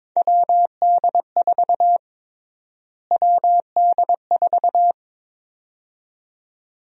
Morse code, heard as a keyed tone, about 22 wpm, 700 Hz. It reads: WD4 WD4
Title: USA Call Sign Prefixes - Repeated 2x - 22wpm